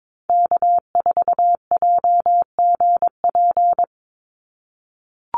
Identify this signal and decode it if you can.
Morse code, heard as a keyed tone, about 22 wpm, 700 Hz. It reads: K4JGP
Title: USA Call Signs 22wpm